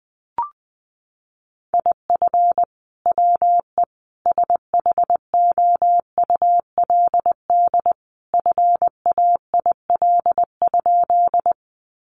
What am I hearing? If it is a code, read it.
Morse code, heard as a keyed tone, about 19 wpm, 700 Hz. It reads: IF WE SHOULD FAIL?
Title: Sentences from Top 1100 Words 20wpm